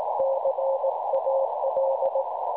Morse code, heard as a keyed tone, about 18 wpm, 540 Hz, with noise. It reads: CAL
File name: cwronde / websdr recording .6kHz